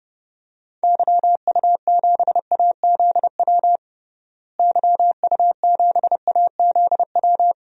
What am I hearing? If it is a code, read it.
Morse code, heard as a keyed tone, about 32 wpm, 700 Hz. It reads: YU7AZW YU7AZW
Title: DXCC-100 Call Signs - 6 Characters - Repeated 3x - 30wpm